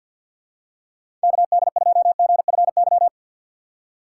Morse code, heard as a keyed tone, about 53 wpm, 700 Hz. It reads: KD2GFY
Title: USA Call Signs - 6 Characters - 3x Character Spacing - 50wpm